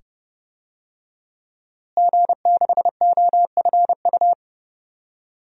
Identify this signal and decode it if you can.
Morse code, heard as a keyed tone, about 30 wpm, 700 Hz. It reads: G6OFU
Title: DXCC-100 Call Signs - 5 Characters - Repeated 2x - 30wpm